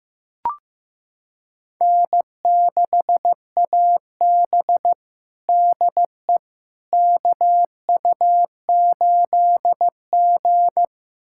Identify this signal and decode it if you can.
Morse code, heard as a keyed tone, about 15 wpm, 700 Hz. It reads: N6AB DE KU8G
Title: USA Callsign DE Callsign - 4 Characters - 15wpm